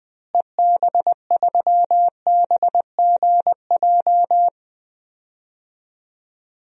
Morse code, heard as a keyed tone, about 20 wpm, 700 Hz. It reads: EB3BGJ
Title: DXCC-100 Call Signs - 6 Characters - Repeated 3x - 20wpm